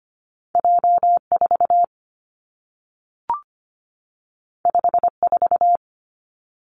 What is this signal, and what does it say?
Morse code, heard as a keyed tone, about 25 wpm, 700 Hz. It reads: J4 54
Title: Two Challenging Letter-Number Combinations 25wpm